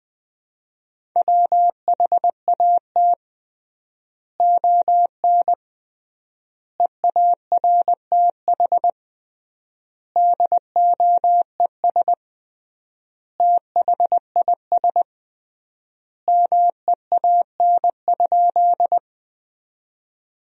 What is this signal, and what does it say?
Morse code, heard as a keyed tone, about 20 wpm, 700 Hz. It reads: WHAT ON EARTH DOES THIS MEAN?
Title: 3x Word-Spacing - Sentences from Top 200 Words - 20wpm